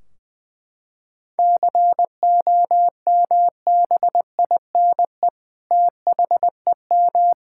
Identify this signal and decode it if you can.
Morse code, heard as a keyed tone, about 20 wpm, 700 Hz. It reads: COMBINE THEM
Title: Sentences from Top 1600 Words 20wpm